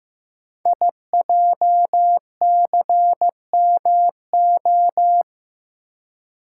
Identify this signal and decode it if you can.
Morse code, heard as a keyed tone, about 15 wpm, 700 Hz. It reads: IJCMO
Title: Code Groups - 5 Letters - 15wpm